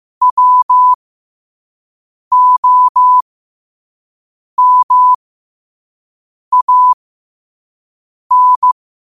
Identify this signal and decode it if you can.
Morse code, heard as a keyed tone, about 15 wpm, spaced as for 5 wpm, 1000 Hz. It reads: WOMAN
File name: qotdfiles / 5wpm